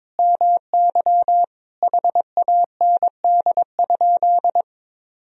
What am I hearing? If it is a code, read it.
Morse code, heard as a keyed tone, about 22 wpm, 700 Hz. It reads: MY HAND?
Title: Sentences from Top 1000 Words - Review Part 4 - 22wpm